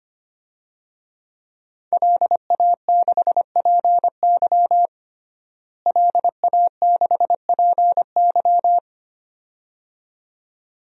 Morse code, heard as a keyed tone, about 25 wpm, 700 Hz. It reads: LA6PY LA6PY
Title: DXCC-100 Call Signs - 5 Characters - Repeated 2x - 25wpm